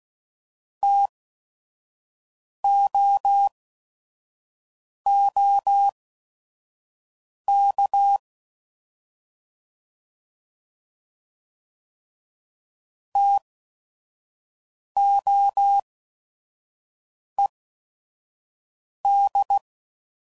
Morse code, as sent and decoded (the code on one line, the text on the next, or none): - --- --- -.- / - --- . -..
TOOK TOED